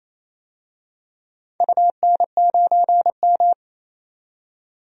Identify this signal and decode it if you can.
Morse code, heard as a keyed tone, about 28 wpm, 700 Hz. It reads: UN9M
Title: DXCC-100 Call Signs - 4 Characters - 28wpm